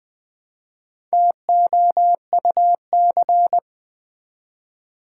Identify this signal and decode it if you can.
Morse code, heard as a keyed tone, about 20 wpm, 700 Hz. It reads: TOUC
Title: Build A Word - 1 Letter At A Time - 5-Letter Words - 20wpm